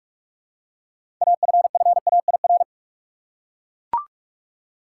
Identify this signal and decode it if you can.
Morse code, heard as a keyed tone, about 42 wpm, 700 Hz. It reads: AFFAIR
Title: Six Letter Words - 45wpm